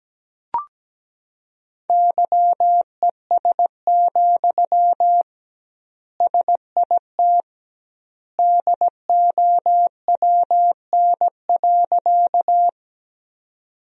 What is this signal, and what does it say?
Morse code, heard as a keyed tone, about 17 wpm, 700 Hz. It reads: YES, SIT DOWN.
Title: 2x Word-Spacing - Sentences from Top 500 Words - 17wpm